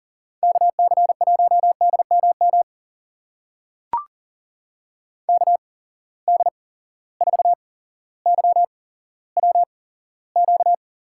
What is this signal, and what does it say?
Morse code, heard as a keyed tone, about 37 wpm, spaced as for 17 wpm, 700 Hz. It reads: KC1DMM KD4YWQ
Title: USA Call Signs - 6 Characters - 8x Character Spacing - 40wpm